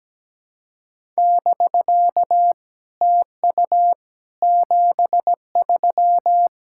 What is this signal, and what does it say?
Morse code, heard as a keyed tone, about 17 wpm, 690 Hz. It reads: <BK> TU 73
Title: SOTA QSO Practice 17wpm